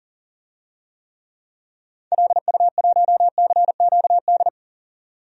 Morse code, heard as a keyed tone, about 40 wpm, 700 Hz. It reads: LU1CQD
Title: DXCC-100 Call Signs - 6 Characters - 40wpm